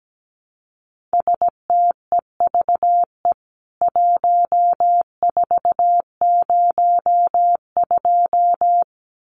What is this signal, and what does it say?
Morse code, heard as a keyed tone, about 17 wpm, 700 Hz. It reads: STEVE 1402
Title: CWT Exchange 17wpm